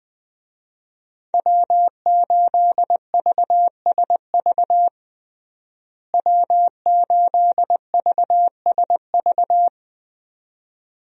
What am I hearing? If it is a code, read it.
Morse code, heard as a keyed tone, about 21 wpm, 700 Hz. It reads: W8VSV W8VSV
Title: USA Call Signs - 5 Characters - Repeated 2x - 20wpm